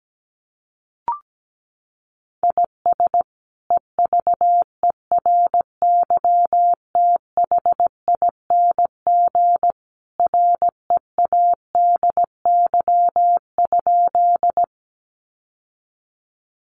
Morse code, heard as a keyed tone, about 17 wpm, 700 Hz. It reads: IS EVERYTHING READY?
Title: Sentences from Top 700 Words 17wpm